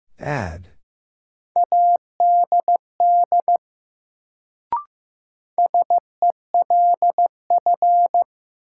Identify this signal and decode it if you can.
Morse code, heard as a keyed tone, about 15 wpm, 700 Hz. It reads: ADD SELF
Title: Sentences from Top 1000 Words - Review Part 1 - New Word Drill - 15wpm